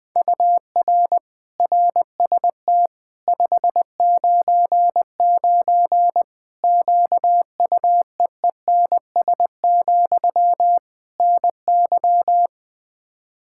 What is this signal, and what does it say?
Morse code, heard as a keyed tone, about 20 wpm, 700 Hz. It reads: UR RST 599 QUEENS, NY
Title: Practice QSO - Exchange 1 - Beginning - 20wpm